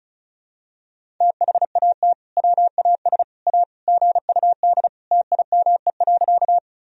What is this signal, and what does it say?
Morse code, heard as a keyed tone, about 36 wpm, 700 Hz. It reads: THAT WAS A GUD TIME.
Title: Sentences from Top 200 Words and Abbreviations 35wpm